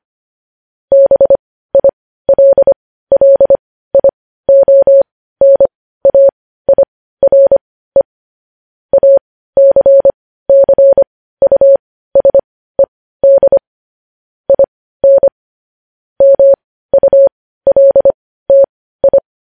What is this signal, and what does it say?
Morse code, heard as a keyed tone, about 25 wpm, spaced as for 15 wpm, 550 Hz. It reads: BILLIONAIRE ACCUSED IN MULTI